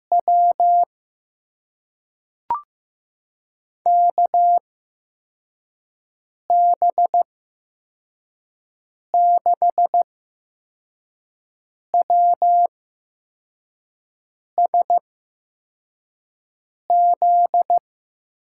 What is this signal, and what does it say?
Morse code, heard as a keyed tone, about 15 wpm, spaced as for 4 wpm, 700 Hz. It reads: W KB6WSZ
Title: USA Call Signs - 8x Character Spacing - 15wpm